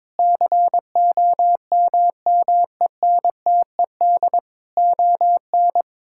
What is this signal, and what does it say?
Morse code, heard as a keyed tone, about 22 wpm, 700 Hz. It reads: COMMENTED ON
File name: Sentences with 7-Word Phrases - 22wpm